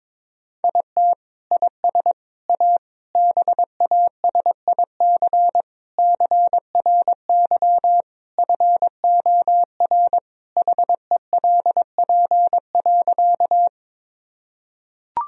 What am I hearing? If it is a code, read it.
Morse code, heard as a keyed tone, about 22 wpm, 700 Hz. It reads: IT IS A BASIC CRY FOR HELP.
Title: Sentences from Top 1000 Words - Review Part 3 - 22wpm